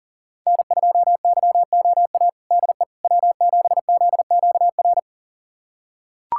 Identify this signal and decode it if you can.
Morse code, heard as a keyed tone, about 40 wpm, 700 Hz. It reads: N2YOA DE W7ZQR
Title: USA Callsign DE Callsign - 5 Characters - 40wpm